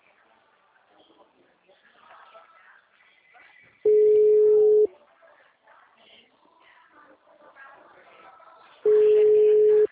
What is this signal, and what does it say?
Ringback tone (Europe, 425 Hz), with speech.
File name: out- -2012- - - .55